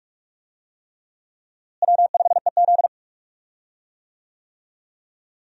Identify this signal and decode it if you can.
Morse code, heard as a keyed tone, about 41 wpm, 700 Hz. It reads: W5EZ
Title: USA Call Signs - 4 Characters - Repeated 3x - 45wpm